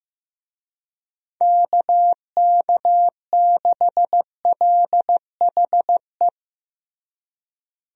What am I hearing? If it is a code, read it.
Morse code, heard as a keyed tone, about 15 wpm, 700 Hz. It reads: KK6LHE
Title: USA Call Signs - 6 Characters - Repeated 2x - 15wpm